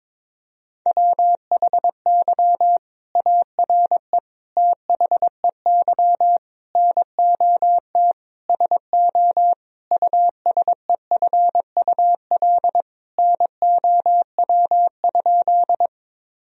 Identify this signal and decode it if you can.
Morse code, heard as a keyed tone, about 21 wpm, 700 Hz. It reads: WHY ARE THEY NOT SO USEFUL NOW?
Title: Sentences from Top 1000 Words - Review Part 4 - 22wpm